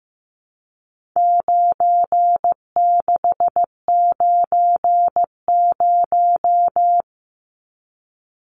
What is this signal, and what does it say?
Morse code, heard as a keyed tone, about 15 wpm, 700 Hz. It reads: 9690
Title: Numbers - 4-Digit 15wpm